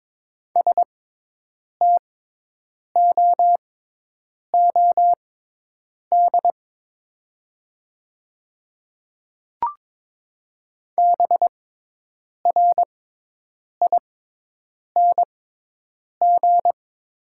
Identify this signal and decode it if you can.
Morse code, heard as a keyed tone, about 22 wpm, spaced as for 8 wpm, 700 Hz. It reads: STOOD BRING
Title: 6x Character Spacing - Top 500 Words - 22wpm